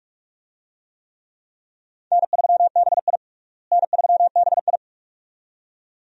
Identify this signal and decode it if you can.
Morse code, heard as a keyed tone, about 46 wpm, 700 Hz. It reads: N3BI N3BI
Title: USA Call Signs - 4 Characters - Repeated 2x - 45wpm